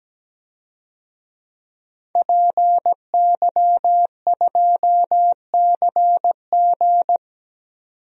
Morse code, heard as a keyed tone, about 17 wpm, 700 Hz. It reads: PY2CG
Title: DXCC-100 Call Signs - 5 Characters - 17wpm